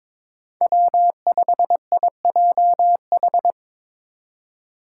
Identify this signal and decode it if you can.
Morse code, heard as a keyed tone, about 22 wpm, 700 Hz. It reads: W5IJH
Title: K1USN SST QSO Practice - 22wpm